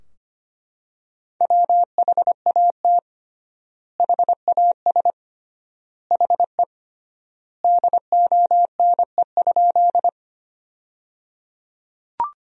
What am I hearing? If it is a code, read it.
Morse code, heard as a keyed tone, about 25 wpm, 700 Hz. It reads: WHAT HAS HE DONE?
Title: 3x Word-Spacing - Sentences from Top 500 Words - 25wpm